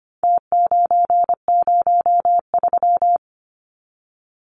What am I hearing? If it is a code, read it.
Morse code, heard as a keyed tone, about 25 wpm, 700 Hz. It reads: T903
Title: Numbers - Contest Style - 4-Digit 25wpm